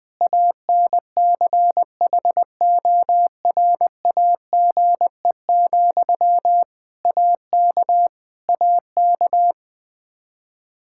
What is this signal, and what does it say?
Morse code, heard as a keyed tone, about 20 wpm, 700 Hz. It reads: ANCHORAGE, AK AK
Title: Practice QSO - Exchange 1 - Beginning - 20wpm